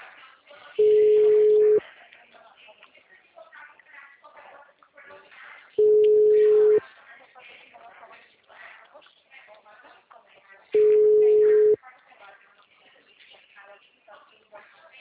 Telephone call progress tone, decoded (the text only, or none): ringback tone